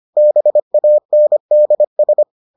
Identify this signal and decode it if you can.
Morse code, heard as a keyed tone, about 25 wpm, 600 Hz. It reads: BANDS